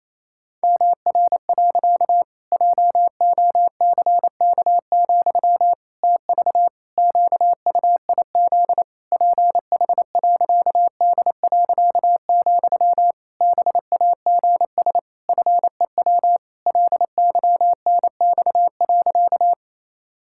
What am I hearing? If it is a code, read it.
Morse code, heard as a keyed tone, about 28 wpm, 700 Hz. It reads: MR. JOCK, TV QUIZ PH.D., BAGS FEW LYNX.
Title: Pangram Challenge v2 - 28wpm